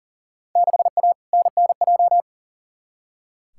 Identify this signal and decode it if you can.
Morse code, heard as a keyed tone, about 37 wpm, 700 Hz. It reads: 6A NNJ
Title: Field Day Exchanges Encore - 40wpm